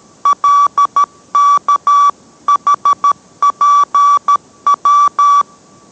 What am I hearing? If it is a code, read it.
Morse code, heard as a keyed tone, about 13 wpm, 1150 Hz. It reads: LKHPW